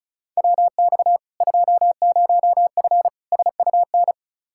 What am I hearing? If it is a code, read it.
Morse code, heard as a keyed tone, about 34 wpm, 700 Hz. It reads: WX 20F SUN
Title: Practice QSO - Exchange 2 - Info - 35wpm